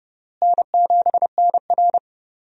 Morse code, heard as a keyed tone, about 30 wpm, 700 Hz. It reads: N7NR
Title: USA Call Signs - 4 Characters - Repeated 2x - 30wpm